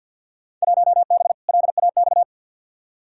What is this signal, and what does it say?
Morse code, heard as a keyed tone, about 53 wpm, 700 Hz. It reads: 1B LAX
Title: Field Day Exchanges - 50wpm